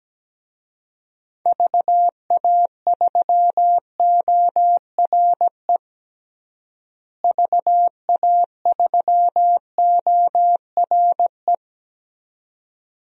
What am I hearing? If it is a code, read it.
Morse code, heard as a keyed tone, about 17 wpm, 700 Hz. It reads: VA3ORE VA3ORE
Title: DXCC-100 Call Signs - 6 Characters - Repeated 2x - 17wpm